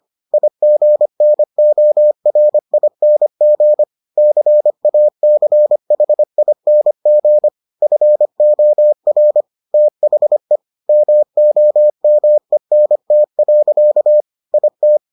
Morse code, heard as a keyed tone, about 25 wpm, 600 Hz. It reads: IGNORING CACHING FOR THE MOMENT. IT